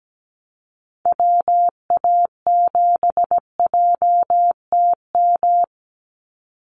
Morse code, heard as a keyed tone, about 17 wpm, 700 Hz. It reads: WA7JTM
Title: NA SOTA Chaser Callsigns 17wpm